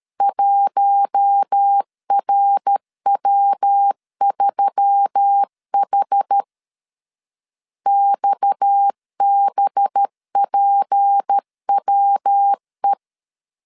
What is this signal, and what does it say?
Morse code, heard as a keyed tone, about 13 wpm, 800 Hz. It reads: 1RW3H XBPWE
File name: LEZIONI / 6 PKX0 / 6 PKX 12wpm 2024